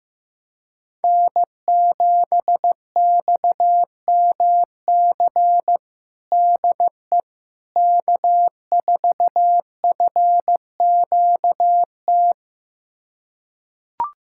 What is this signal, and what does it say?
Morse code, heard as a keyed tone, about 16 wpm, 700 Hz. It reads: N7XMC DE K4FQT
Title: USA Callsign DE Callsign 15wpm